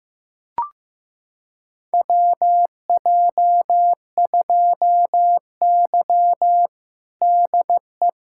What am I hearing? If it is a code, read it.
Morse code, heard as a keyed tone, about 15 wpm, 700 Hz. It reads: WJ2Y DE
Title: USA Callsign DE Callsign - 4 Characters - 15wpm